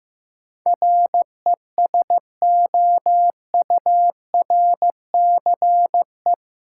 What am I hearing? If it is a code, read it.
Morse code, heard as a keyed tone, about 15 wpm, 700 Hz. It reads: RESOURCE
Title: Sentences from Top 1300 Words - New Word Drill - 15wpm